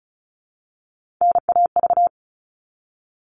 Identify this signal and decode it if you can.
Morse code, heard as a keyed tone, about 34 wpm, 700 Hz. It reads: NAV
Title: All Cut Numbers - 3-Digit 35wpm